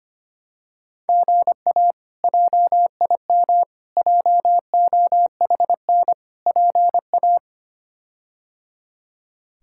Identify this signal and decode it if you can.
Morse code, heard as a keyed tone, about 25 wpm, 700 Hz. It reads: GA JIM JOHN PA
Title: K1USN SST QSO Practice - 25wpm